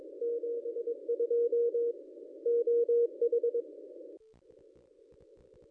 Morse code, heard as a keyed tone, about 22 wpm, 470 Hz. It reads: 72 OH
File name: contests / ss-cw-k5tr-op-n5rz -1400